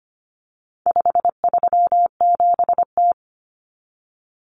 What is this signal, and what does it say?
Morse code, heard as a keyed tone, about 25 wpm, 700 Hz. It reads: 537T
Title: Common Cut Numbers - 4-Digit 25wpm